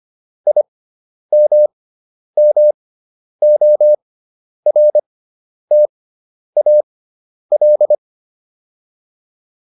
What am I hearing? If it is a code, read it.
Morse code, heard as a keyed tone, about 25 wpm, spaced as for 10 wpm, 600 Hz. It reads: IMMORTAL